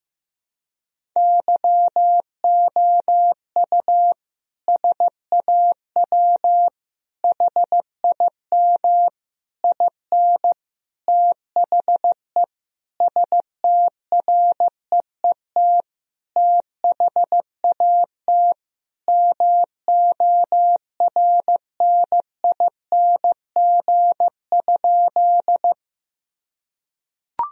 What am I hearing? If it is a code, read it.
Morse code, heard as a keyed tone, about 15 wpm, 700 Hz. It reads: YOU SAW HIM IN THE STREET THAT MORNING?
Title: Sentences from Top 1000 Words - Review Part 3 - 15wpm